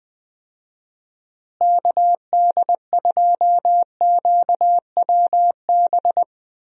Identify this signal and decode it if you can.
Morse code, heard as a keyed tone, about 20 wpm, 700 Hz. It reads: KD2QWB
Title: USA Call Signs - 6 Characters - Repeated 3x - 20wpm